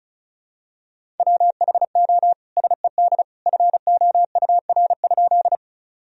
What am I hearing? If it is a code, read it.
Morse code, heard as a keyed tone, about 36 wpm, 700 Hz. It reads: WHO SED FOUR?
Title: Sentences from Top 300 Words and Abbreviations 35wpm